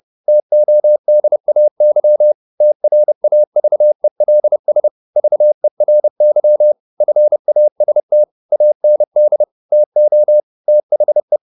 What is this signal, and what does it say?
Morse code, heard as a keyed tone, about 30 wpm, 600 Hz. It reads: TODAY TRAVELS VERY FAST AND TO THE
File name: the old man and the sea / the old man and the sea 1 30wpm